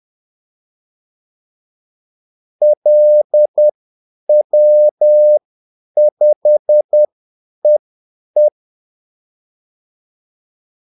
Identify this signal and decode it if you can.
Morse code, heard as a keyed tone, about 10 wpm, spaced as for 8 wpm, 600 Hz. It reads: LW5EE
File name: Callsigns-10-8wpm